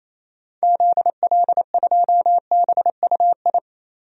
Morse code, heard as a keyed tone, about 27 wpm, 700 Hz. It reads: ZL2BUI
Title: DXCC-100 Call Signs - 6 Characters - Repeated 2x - 28wpm